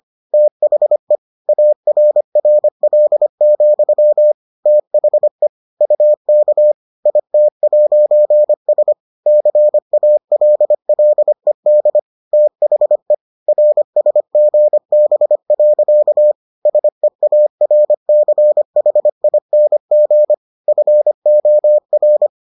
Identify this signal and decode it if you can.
Morse code, heard as a keyed tone, about 25 wpm, 600 Hz. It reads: THE ARRL, THE UK IT'S CALLED THE RSGB. SEARCHING FOR